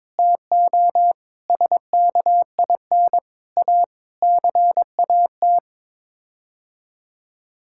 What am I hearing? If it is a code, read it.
Morse code, heard as a keyed tone, about 22 wpm, 700 Hz. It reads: TO SKIN A CAT
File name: Sets of 9 Words - 22wpm